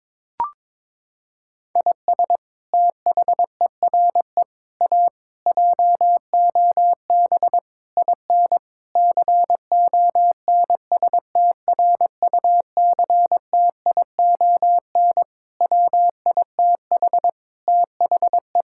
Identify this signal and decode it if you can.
Morse code, heard as a keyed tone, about 22 wpm, 700 Hz. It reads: IS THERE A JOB IN CONSTRUCTION WITH THE
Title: Sentences from Top 1600 Words 22wpm